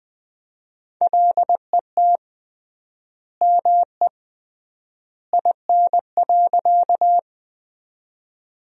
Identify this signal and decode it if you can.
Morse code, heard as a keyed tone, about 20 wpm, 700 Hz. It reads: LET ME IN.
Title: 3x Word-Spacing - Sentences from Top 300 Words - 20wpm